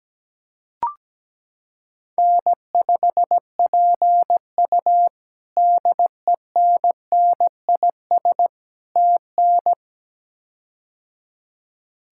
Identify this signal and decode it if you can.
Morse code, heard as a keyed tone, about 17 wpm, 700 Hz. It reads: N5PU DENNIS TN
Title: K1USN SST QSO Practice - 17wpm